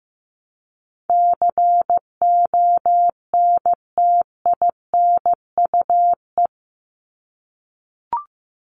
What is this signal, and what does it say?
Morse code, heard as a keyed tone, about 15 wpm, 700 Hz. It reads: CONTINUE
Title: ED Suffix - Top 100 Words with Base Words 15wpm